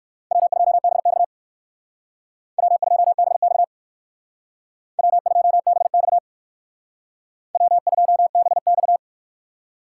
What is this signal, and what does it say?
Morse code, heard as a keyed tone, about 67 wpm, 700 Hz. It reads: W2BX W2BX W2BX W2BX